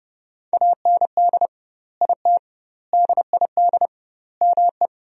Morse code, heard as a keyed tone, about 30 wpm, 700 Hz. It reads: AND IT DID ME
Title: 2x Word-Spacing - Sentences from Top 200 Words - 30wpm